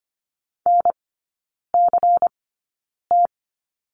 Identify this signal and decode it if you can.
Morse code, heard as a keyed tone, about 25 wpm, spaced as for 9 wpm, 700 Hz. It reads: NCT
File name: course / 25:10wpm / 093 Lesson 16e - Letter c - Three Characters 25wpm